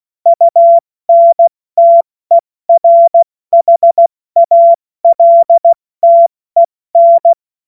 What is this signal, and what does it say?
Morse code, heard as a keyed tone, about 16 wpm, 680 Hz. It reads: UNTERHALTEN
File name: Witz-14-Oktober-2025-70-F80